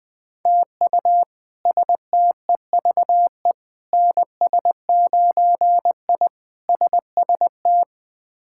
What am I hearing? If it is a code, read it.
Morse code, heard as a keyed tone, about 20 wpm, 700 Hz. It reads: TU STEVE NS9I SST
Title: K1USN SST QSO Practice - 20wpm